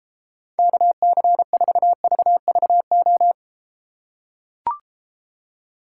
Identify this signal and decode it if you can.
Morse code, heard as a keyed tone, about 33 wpm, 700 Hz. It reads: KC4VVO